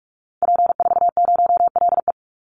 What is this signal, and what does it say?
Morse code, heard as a keyed tone, about 44 wpm, 700 Hz. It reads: P40LE
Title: Call Signs - 45wpm